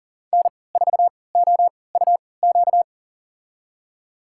Morse code, heard as a keyed tone, about 40 wpm, spaced as for 23 wpm, 700 Hz. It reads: N4OUQ
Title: USA Call Signs - 5 Characters - 3x Character Spacing - 40wpm